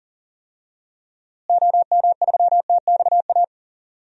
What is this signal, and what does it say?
Morse code, heard as a keyed tone, about 40 wpm, 700 Hz. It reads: OM3TXA
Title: DXCC-100 Call Signs - 6 Characters - Repeated 2x - 40wpm